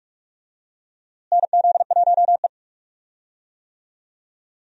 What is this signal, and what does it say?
Morse code, heard as a keyed tone, about 46 wpm, 700 Hz. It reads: NZ1E
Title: USA Call Signs - 4 Characters - Repeated 2x - 45wpm